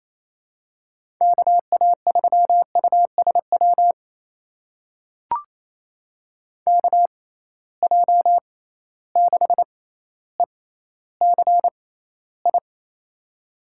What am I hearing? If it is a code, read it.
Morse code, heard as a keyed tone, about 28 wpm, spaced as for 14 wpm, 700 Hz. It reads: KA3USW KJ6ECI
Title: USA Call Signs - 6 Characters - 6x Character Spacing - 28wpm